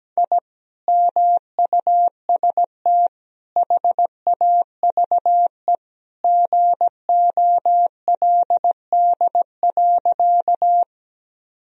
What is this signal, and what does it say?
Morse code, heard as a keyed tone, about 17 wpm, 700 Hz. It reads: I MUST HAVE GOLD.
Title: Sentences from Top 1000 Words - Review Part 3 - 17wpm